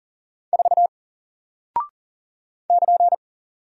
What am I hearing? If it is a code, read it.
Morse code, heard as a keyed tone, about 40 wpm, 700 Hz. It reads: 4 <KN>
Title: Single Character (Letter:Number:Punctuation:Prosign) Alt 40wpm